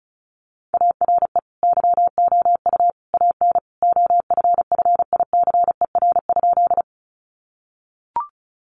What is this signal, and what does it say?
Morse code, heard as a keyed tone, about 34 wpm, 700 Hz. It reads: ARE YOU AN OFFICER?
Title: Sentences from Top 1200 Words 35wpm